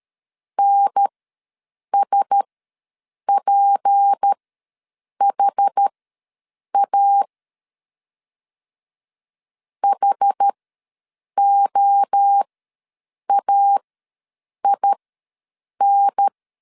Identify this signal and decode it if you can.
Morse code, heard as a keyed tone, about 13 wpm, spaced as for 7 wpm, 800 Hz. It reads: NSPHA HOAIN
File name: LEZIONI / 6 PKX0 / 6 P ws 12wpm 3117